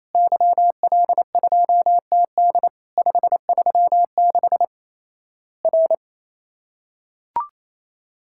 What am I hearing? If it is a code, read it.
Morse code, heard as a keyed tone, about 28 wpm, 700 Hz. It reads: YL2TD 536 R
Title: Worked All Europe DX Contest - QTC Exchange - 28wpm